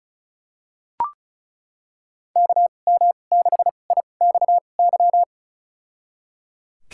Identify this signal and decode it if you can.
Morse code, heard as a keyed tone, about 34 wpm, spaced as for 25 wpm, 700 Hz. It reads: KM6IXY
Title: USA Call Signs - 6 Characters - 2x Character Spacing - 35wpm